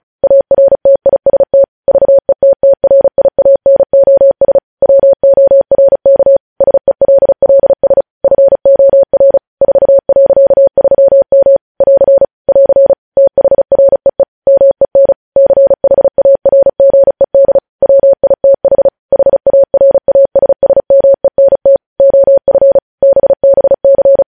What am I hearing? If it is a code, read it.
Morse code, heard as a keyed tone, about 36 wpm, 550 Hz. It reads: ARTIST VETTRIANOS WORK SELLS FOR 4.3M <AR> <AR> THREE MEN CHARGED WITH HARASSMENT OF BBC
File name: podcast / News headlines-1x-35x35